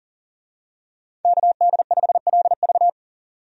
Morse code, heard as a keyed tone, about 40 wpm, 700 Hz. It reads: KD5LV
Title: USA Call Signs - 5 Characters - 40wpm